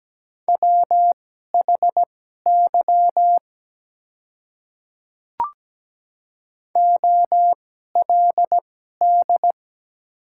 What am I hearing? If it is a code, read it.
Morse code, heard as a keyed tone, about 17 wpm, spaced as for 12 wpm, 700 Hz. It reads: WHY OLD
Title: 2x Character Spacing - Top 200 Words - 17wpm